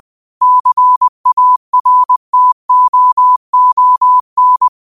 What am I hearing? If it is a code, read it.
Morse code, heard as a keyed tone, about 20 wpm, 1000 Hz. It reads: CARTOON
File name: qotdfiles / 20wpm